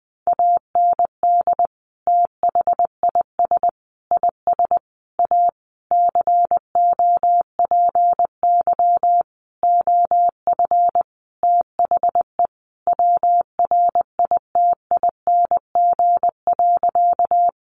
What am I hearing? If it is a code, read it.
Morse code, heard as a keyed tone, about 20 wpm, 700 Hz. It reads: AND THIS IS A COPY OF THE WRITING.
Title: Sentences from Top 1400 Words 20wpm